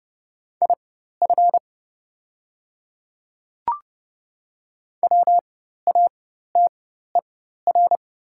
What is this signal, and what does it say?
Morse code, heard as a keyed tone, about 30 wpm, spaced as for 14 wpm, 700 Hz. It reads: IF WATER
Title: 4x Character Spacing - Top 100 Words - 30wpm